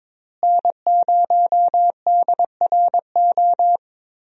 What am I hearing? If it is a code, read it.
Morse code, heard as a keyed tone, about 22 wpm, 700 Hz. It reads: N0DRO
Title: USA Call Signs - 5 Characters - Repeated 3x - 22wpm